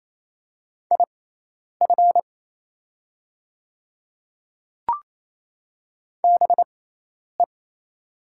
Morse code, heard as a keyed tone, about 28 wpm, spaced as for 10 wpm, 700 Hz. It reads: IF BE
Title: 6x Character Spacing - Top 200 Words - 28wpm